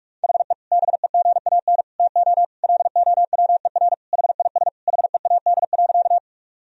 Morse code, heard as a keyed tone, about 41 wpm, 700 Hz. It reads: HE BEGAN TO LOWER HIS HEAD.
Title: Sentences from Top 1000 Words - Review Part 4 - 45wpm